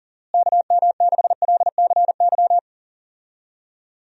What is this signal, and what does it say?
Morse code, heard as a keyed tone, about 37 wpm, 700 Hz. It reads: KM6LCY
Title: USA Call Signs - 6 Characters - Repeated 2x - 40wpm